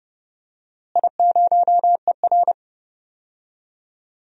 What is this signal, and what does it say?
Morse code, heard as a keyed tone, about 30 wpm, 700 Hz. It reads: I0ER
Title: DXCC-100 Call Signs - 4 Characters - Repeated 2x - 30wpm